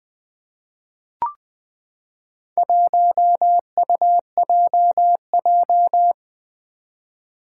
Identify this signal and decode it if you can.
Morse code, heard as a keyed tone, about 20 wpm, 700 Hz. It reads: 1UJJ
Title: Four Challenging Letter-Number Combinations 20wpm